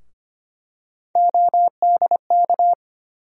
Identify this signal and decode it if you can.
Morse code, heard as a keyed tone, about 25 wpm, 700 Hz. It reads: ODK
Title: USA Call Sign Postfixes - 3 Letters 25wpm